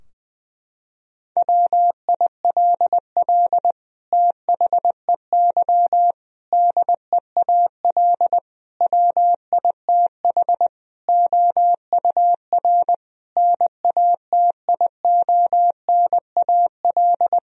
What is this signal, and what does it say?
Morse code, heard as a keyed tone, about 20 wpm, 700 Hz. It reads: WILL THEY DEAL WITH OUR NATIONAL
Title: Sentences from Top 1000 Words - Review Part 4 - 20wpm